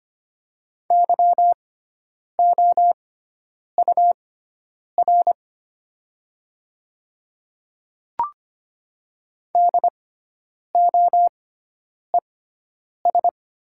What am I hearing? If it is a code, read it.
Morse code, heard as a keyed tone, about 25 wpm, spaced as for 9 wpm, 700 Hz. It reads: YOUR DOES
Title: 6x Character Spacing - Top 200 Words - 25wpm